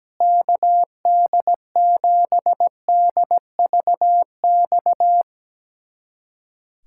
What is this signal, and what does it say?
Morse code, heard as a keyed tone, about 17 wpm, 700 Hz. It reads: KD7DVX
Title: USA Callsign DE Callsign - 6 Characters - 17wpm